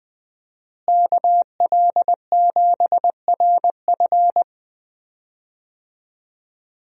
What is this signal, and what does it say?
Morse code, heard as a keyed tone, about 20 wpm, 700 Hz. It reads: KL7RF
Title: DXCC-100 Call Signs - 5 Characters - Repeated 2x - 20wpm